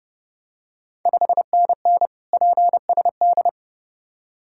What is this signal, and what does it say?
Morse code, heard as a keyed tone, about 30 wpm, 700 Hz. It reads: 5NN PSD
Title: Ontario QSO Party Exchange 30wpm